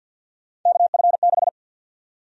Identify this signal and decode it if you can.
Morse code, heard as a keyed tone, about 48 wpm, 700 Hz. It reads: KF6
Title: USA Call Sign Prefixes - 50wpm